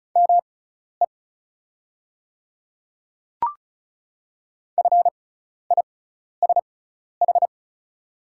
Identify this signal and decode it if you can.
Morse code, heard as a keyed tone, about 34 wpm, spaced as for 12 wpm, 690 Hz. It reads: ME FISH
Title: 6x Character Spacing - Top 300 Words - 35wpm